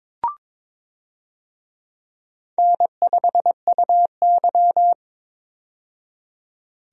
Morse code, heard as a keyed tone, about 22 wpm, 700 Hz. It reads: N5UY
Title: USA Call Signs - 4 Characters - 22wpm